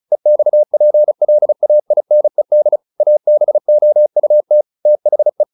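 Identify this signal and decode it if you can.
Morse code, heard as a keyed tone, about 34 wpm, 600 Hz. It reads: EXPLAINED ABOUT THE
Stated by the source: Morse practice at 35 wpm from CW med Gnister, sending Ken Follett's novel Never in English.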